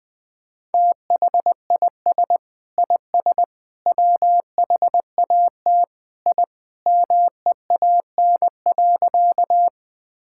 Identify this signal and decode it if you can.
Morse code, heard as a keyed tone, about 21 wpm, 700 Hz. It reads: THIS IS WHAT I MEAN.
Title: Sentences with 3-Word Phrases - 20wpm